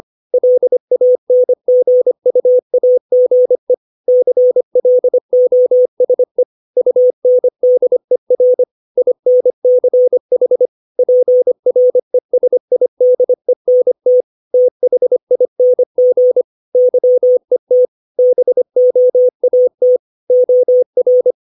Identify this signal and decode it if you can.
Morse code, heard as a keyed tone, about 25 wpm, 500 Hz. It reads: LANGUAGE CLOSE UNDER INCH PRESIDENT THING YET BOAT OR
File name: 632 words / 632 Words 25WPM